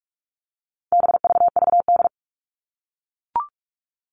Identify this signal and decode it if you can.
Morse code, heard as a keyed tone, about 44 wpm, 700 Hz. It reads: BVVD
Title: All Cut Numbers - 4-Digit 45wpm